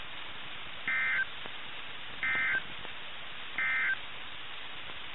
Emergency Alert System (SAME) tones, heard as harsh buzzing bursts. It is heard from 0.8 to 4.0 s.
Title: NOAA Weather Radio - KIH61